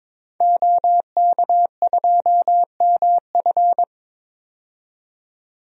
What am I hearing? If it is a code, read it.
Morse code, heard as a keyed tone, about 22 wpm, 700 Hz. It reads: OK2MF
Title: DXCC-100 Call Signs - 5 Characters - 22wpm